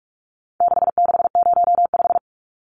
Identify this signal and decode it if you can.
Morse code, heard as a keyed tone, about 44 wpm, 700 Hz. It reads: 6605
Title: Numbers - 4-Digit 45wpm